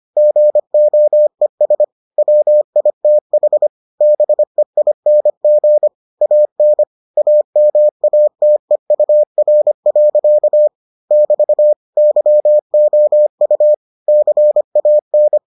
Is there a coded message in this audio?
Yes: Morse code from 0.2 to 15.4 s, heard as a keyed tone, about 25 wpm, 600 Hz. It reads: GOES WITH BEING AN AMATEUR. <BT> YOU CAN